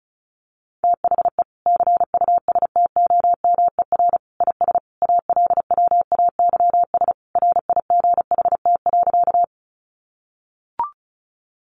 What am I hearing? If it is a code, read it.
Morse code, heard as a keyed tone, about 34 wpm, 700 Hz. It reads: THE CUSTOMER IS ALWAYS RIGHT.
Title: Sentences from Top 1100 Words 35wpm